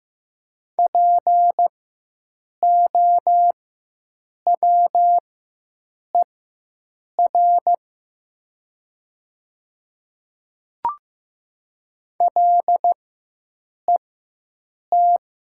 Morse code, heard as a keyed tone, about 15 wpm, spaced as for 7 wpm, 700 Hz. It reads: POWER LET
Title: 4x Character Spacing - Top 500 Words - 15wpm